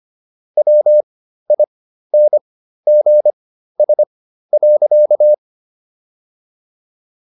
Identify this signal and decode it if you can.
Morse code, heard as a keyed tone, about 25 wpm, spaced as for 13 wpm, 600 Hz. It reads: WINGS.